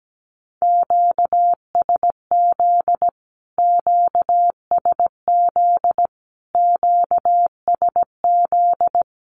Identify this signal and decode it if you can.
Morse code, heard as a keyed tone, about 17 wpm, 700 Hz. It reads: QSZ QSZ QSZ
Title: Q-Codes Repeated 3x 17wpm